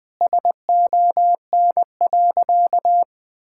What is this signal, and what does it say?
Morse code, heard as a keyed tone, about 20 wpm, 700 Hz. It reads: SON.
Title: Sentences with 8-Word Phrases - 20wpm